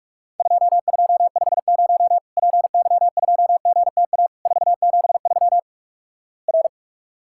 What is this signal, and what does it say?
Morse code, heard as a keyed tone, about 48 wpm, 700 Hz. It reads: 2250 PY2GTA 473 R
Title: Worked All Europe DX Contest - QTC Exchange - 45wpm